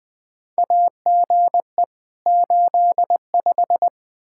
Morse code, heard as a keyed tone, about 20 wpm, 700 Hz. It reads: AGE 85
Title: Practice QSO - Exchange 3 - More Info - 20wpm